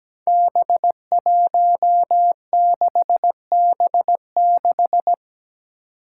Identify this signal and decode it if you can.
Morse code, heard as a keyed tone, about 17 wpm, 700 Hz. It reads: B16B6
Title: Five Challenging Letter-Number Combinations 17wpm